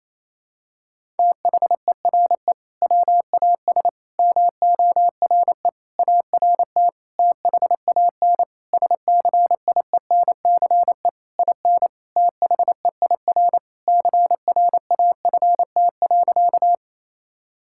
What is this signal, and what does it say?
Morse code, heard as a keyed tone, about 27 wpm, 700 Hz. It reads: THERE WAS MORE ART THAN SCIENCE IN THEIR CRAFT.
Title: Sentences from Top 1000 Words - Review Part 6 - 28wpm